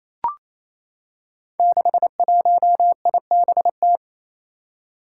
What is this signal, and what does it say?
Morse code, heard as a keyed tone, about 27 wpm, 700 Hz. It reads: 61IBT
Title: Five Challenging Letter-Number Combinations 28wpm